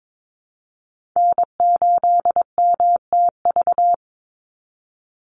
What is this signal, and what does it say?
Morse code, heard as a keyed tone, about 22 wpm, 700 Hz. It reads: N8MTV
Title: Call Signs - 22wpm